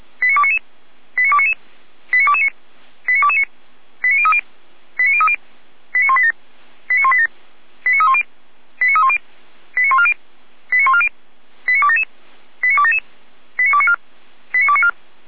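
Five-tone selective call (ZVEI), heard as quick runs of five tones, tones of about 70 ms. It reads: 89299 89299 89229 89229 89030 89030 89187 89187 89310 89310 89140 89140 89277 89277 89294 89294
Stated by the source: ZVEI five-tone selective call received with SDR.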